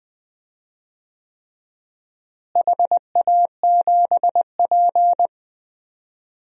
Morse code, heard as a keyed tone, about 20 wpm, 700 Hz. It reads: HA7P
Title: DXCC-100 Call Signs - 4 Characters - Repeated 3x - 20wpm